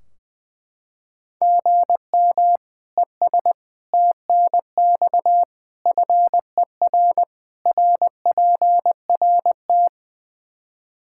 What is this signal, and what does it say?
Morse code, heard as a keyed tone, about 20 wpm, 700 Hz. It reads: GM ES TNX FER RPRT
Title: K1USN SST QSO Practice 20wpm